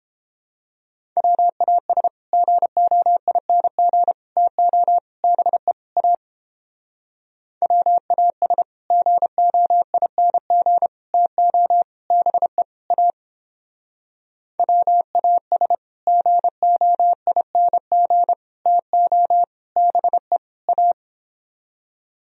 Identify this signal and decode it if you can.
Morse code, heard as a keyed tone, about 29 wpm, 700 Hz. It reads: WAS GOING TO BE A WAS GOING TO BE A WAS GOING TO BE A